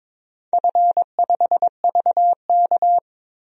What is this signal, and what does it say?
Morse code, heard as a keyed tone, about 22 wpm, 700 Hz. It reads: F5VK
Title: DXCC-100 Call Signs - 4 Characters - 22wpm